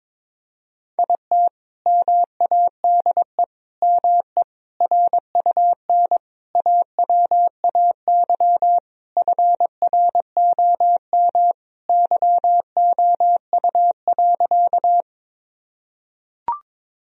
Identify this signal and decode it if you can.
Morse code, heard as a keyed tone, about 22 wpm, 700 Hz. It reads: IT MADE ME RUN AWAY FROM YOU.
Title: Sentences from Top 1000 Words - Review Part 1 - 22wpm